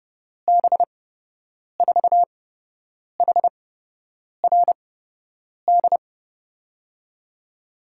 Morse code, heard as a keyed tone, about 30 wpm, spaced as for 8 wpm, 700 Hz. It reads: B4HRD
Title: USA Call Signs - 6 Characters - 8x Character Spacing - 30wpm